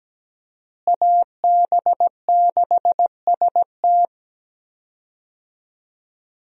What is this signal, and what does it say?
Morse code, heard as a keyed tone, about 17 wpm, 700 Hz. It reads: AB6ST
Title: USA Call Signs - 5 Characters - Repeated 3x - 17wpm